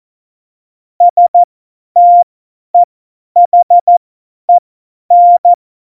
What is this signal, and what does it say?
Morse code, heard as a keyed tone, about 14 wpm, spaced as for 10 wpm, 700 Hz. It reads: STEHEN
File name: Morsekurs / Witze / Witz 28. November 2017, 50 F70